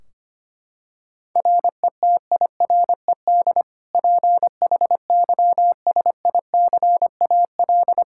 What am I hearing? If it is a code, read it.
Morse code, heard as a keyed tone, about 25 wpm, 700 Hz. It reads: RETIRED PHYSICAL
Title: K1USN SST QSO Practice 25wpm